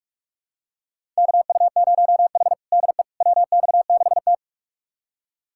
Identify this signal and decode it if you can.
Morse code, heard as a keyed tone, about 46 wpm, 700 Hz. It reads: KU0H DE WX6T
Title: USA Callsign DE Callsign - 4 Characters - 45wpm